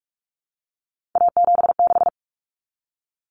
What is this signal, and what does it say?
Morse code, heard as a keyed tone, about 44 wpm, 700 Hz. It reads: A76
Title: Common Cut Numbers - 3-Digit 45wpm